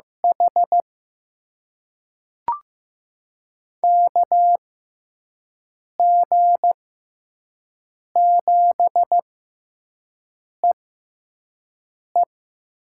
Morse code, heard as a keyed tone, about 16 wpm, spaced as for 5 wpm, 700 Hz. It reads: H KG7EE
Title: USA Call Signs - 6x Character Spacing - 15wpm